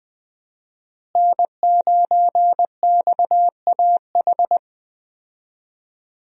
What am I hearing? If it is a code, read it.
Morse code, heard as a keyed tone, about 20 wpm, 700 Hz. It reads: N9XAH
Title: USA Call Signs - 5 Characters - Repeated 2x - 20wpm